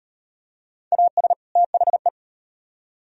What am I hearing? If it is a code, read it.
Morse code, heard as a keyed tone, about 38 wpm, 700 Hz. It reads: AS THE